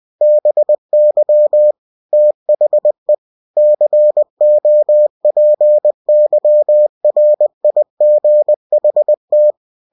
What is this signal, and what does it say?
Morse code, heard as a keyed tone, about 20 wpm, 600 Hz. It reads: BY THE COPYRIGHT
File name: cw-books / winnie-the-pooh / winnie-20wpm-page-0115